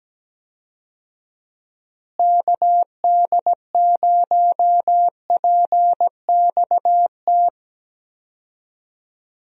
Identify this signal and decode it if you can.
Morse code, heard as a keyed tone, about 17 wpm, 700 Hz. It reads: KD0PXT
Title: USA Call Signs - 6 Characters - Repeated 2x - 17wpm